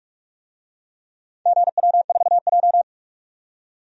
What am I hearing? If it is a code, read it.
Morse code, heard as a keyed tone, about 44 wpm, 700 Hz. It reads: GW4J
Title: DXCC-100 Call Signs - 4 Characters - 45wpm